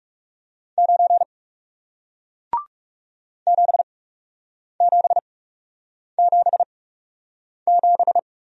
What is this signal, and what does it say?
Morse code, heard as a keyed tone, about 39 wpm, spaced as for 8 wpm, 700 Hz. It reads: 9 7777